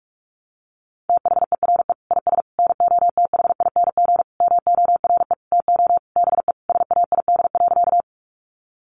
Morse code, heard as a keyed tone, about 44 wpm, 700 Hz. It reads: THERE IS NOTHING MORE TO BE SAID.
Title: Sentences from Top 500 Words 45wpm